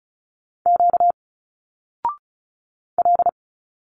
Morse code, heard as a keyed tone, about 34 wpm, 700 Hz. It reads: Q L
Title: Single Letters 35wpm